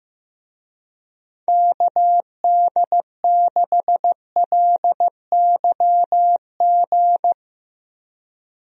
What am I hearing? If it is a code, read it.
Morse code, heard as a keyed tone, about 15 wpm, 700 Hz. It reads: KD6LYG
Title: USA Call Signs - 6 Characters - Repeated 2x - 15wpm